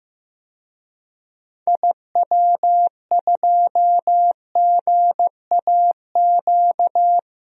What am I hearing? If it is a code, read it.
Morse code, heard as a keyed tone, about 15 wpm, 700 Hz. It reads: IW2GAQ
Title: DXCC-100 Call Signs - 6 Characters - Repeated 3x - 15wpm